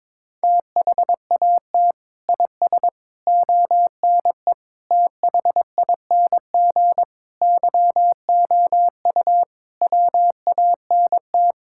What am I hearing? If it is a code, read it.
Morse code, heard as a keyed tone, about 22 wpm, 700 Hz. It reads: THAT IS ONE THING YOU WANT
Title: Sentences from Top 1600 Words 22wpm